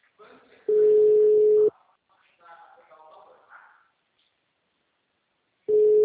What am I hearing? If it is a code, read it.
Ringback tone (Europe, 425 Hz), with speech.
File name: out- -2027- - - .15